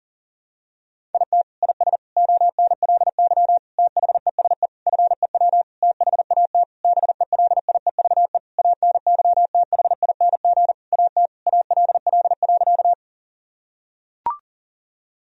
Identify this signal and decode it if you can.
Morse code, heard as a keyed tone, about 40 wpm, 700 Hz. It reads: IT IS ONLY THESE FEW THAT BELIEVE ANYTHING AT ALL.
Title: Sentences from Top 1000 Words - Review Part 1 - 40wpm